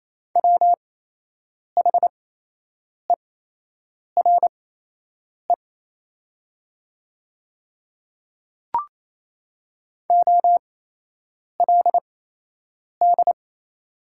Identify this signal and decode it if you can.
Morse code, heard as a keyed tone, about 27 wpm, spaced as for 8 wpm, 700 Hz. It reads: WHERE OLD
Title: 8x Character Spacing - Top 200 Words - 28wpm